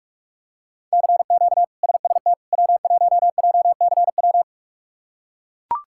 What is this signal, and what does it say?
Morse code, heard as a keyed tone, about 42 wpm, 700 Hz. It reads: CQ SST W1JCW
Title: K1USN SST QSO Practice - 45wpm